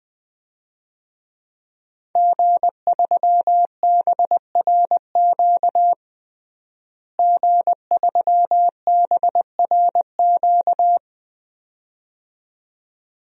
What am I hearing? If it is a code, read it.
Morse code, heard as a keyed tone, about 21 wpm, 700 Hz. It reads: G3BRQ G3BRQ
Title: DXCC-100 Call Signs - 5 Characters - Repeated 2x - 20wpm